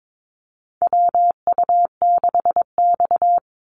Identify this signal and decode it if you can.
Morse code, heard as a keyed tone, about 22 wpm, 700 Hz. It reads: WU6X
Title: Call Signs Repeated 2x - 22wpm